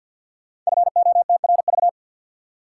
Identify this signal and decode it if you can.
Morse code, heard as a keyed tone, about 53 wpm, 700 Hz. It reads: FOTRV
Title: Code Groups - 5 Letters - 50wpm